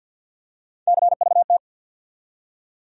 Morse code, heard as a keyed tone, about 51 wpm, 700 Hz. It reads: CVT
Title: USA Call Sign Postfixes - 3 Letters 50wpm